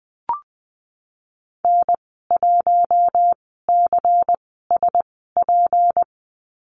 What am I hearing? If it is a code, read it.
Morse code, heard as a keyed tone, about 20 wpm, spaced as for 15 wpm, 700 Hz. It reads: N1CSP
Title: 161 Lesson 26g - Letter g - Call Signs 20wpm